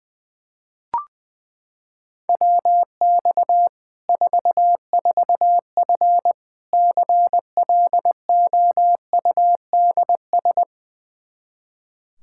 Morse code, heard as a keyed tone, about 21 wpm, 700 Hz. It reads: WX 44F CLOUDS
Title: K1USN SST QSO Practice 20wpm